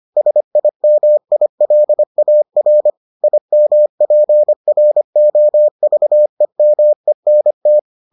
Morse code, heard as a keyed tone, about 25 wpm, 600 Hz. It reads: SIMILAR IMPROVEMENT